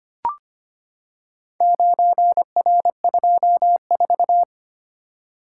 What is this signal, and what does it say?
Morse code, heard as a keyed tone, about 25 wpm, 700 Hz. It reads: 9R24
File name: practice / 25wpm / Four Letter-Number Combinations - 25wpm